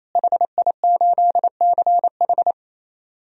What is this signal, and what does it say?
Morse code, heard as a keyed tone, about 28 wpm, 700 Hz. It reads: HI8CH
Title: DXCC-100 Call Signs - 5 Characters - Repeated 3x - 28wpm